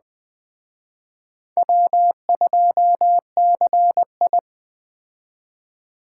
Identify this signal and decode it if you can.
Morse code, heard as a keyed tone, about 20 wpm, 700 Hz. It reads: W2CI
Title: Speed Racing - USA 1x2 Call Signs 17wpm